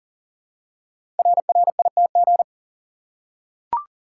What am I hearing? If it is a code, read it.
Morse code, heard as a keyed tone, about 39 wpm, 700 Hz. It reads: RRITG
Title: Code Groups - 5 Letter:Numbers - 40wpm